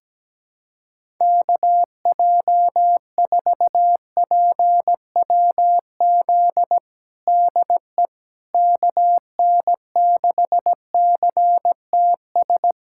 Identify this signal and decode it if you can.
Morse code, heard as a keyed tone, about 17 wpm, 700 Hz. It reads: KJ4PWZ DE KN6CTS
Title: USA Callsign DE Callsign 17wpm